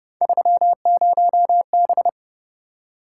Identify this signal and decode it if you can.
Morse code, heard as a keyed tone, about 29 wpm, 700 Hz. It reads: 30B